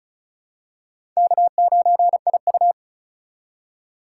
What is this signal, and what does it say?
Morse code, heard as a keyed tone, about 36 wpm, 700 Hz. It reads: K9IU
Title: USA Call Signs - 4 Characters - 3x Character Spacing - 35wpm